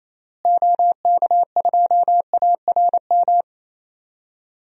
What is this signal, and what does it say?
Morse code, heard as a keyed tone, about 28 wpm, 700 Hz. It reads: OK2ARM
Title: DXCC-100 Call Signs - 6 Characters - Repeated 3x - 28wpm